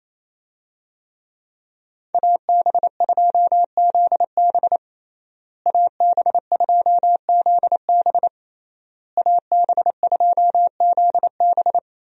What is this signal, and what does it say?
Morse code, heard as a keyed tone, about 27 wpm, 700 Hz. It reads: AB2ZB AB2ZB AB2ZB
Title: USA Call Signs - 5 Characters - Repeated 3x - 28wpm